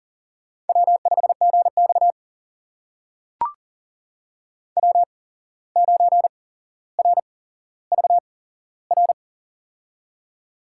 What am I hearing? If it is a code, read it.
Morse code, heard as a keyed tone, about 40 wpm, spaced as for 16 wpm, 700 Hz. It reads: W5GX W9RVR
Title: USA Call Signs - 8x Character Spacing - 40wpm